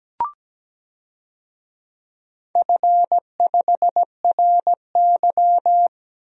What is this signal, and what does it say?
Morse code, heard as a keyed tone, about 17 wpm, 700 Hz. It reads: F5RY
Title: DXCC-100 Call Signs - 4 Characters - Repeated 2x - 17wpm